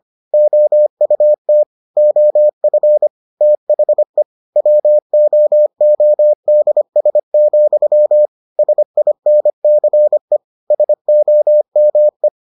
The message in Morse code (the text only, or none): OUT OF THE WOODS, SINCE SOME